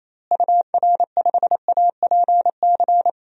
Morse code, heard as a keyed tone, about 28 wpm, 700 Hz. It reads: UR5APC
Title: DXCC-100 Call Signs - 6 Characters - Repeated 2x - 28wpm